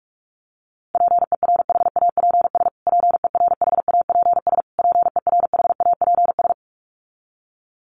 Morse code, heard as a keyed tone, about 44 wpm, 700 Hz. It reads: PERHAPS PERHAPS PERHAPS
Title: Top 500 Words Repeated 45wpm